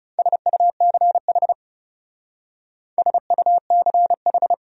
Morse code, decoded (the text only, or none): SUCH SUCH